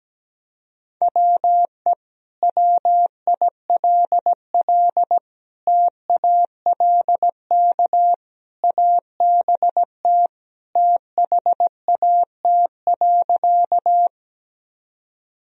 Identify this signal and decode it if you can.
Morse code, heard as a keyed tone, about 17 wpm, 700 Hz. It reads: WE WILL TALK ABT THAT.
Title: Sentences from Top 400 Words and Abbreviations 17wpm